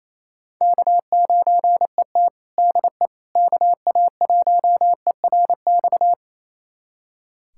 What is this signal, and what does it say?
Morse code, heard as a keyed tone, about 28 wpm, 700 Hz. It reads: K9ET DE KA1ERX
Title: USA Callsign DE Callsign 28wpm